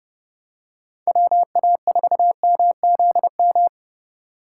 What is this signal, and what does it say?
Morse code, heard as a keyed tone, about 32 wpm, 700 Hz. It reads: WA4MZM
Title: USA Call Signs - 6 Characters - 2x Character Spacing - 30wpm